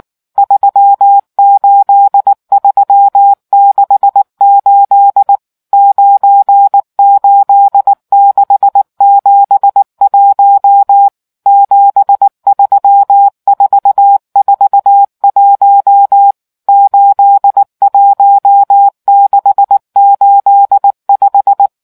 Morse code, heard as a keyed tone, about 19 wpm, 800 Hz. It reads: 38368 98671 73441 81685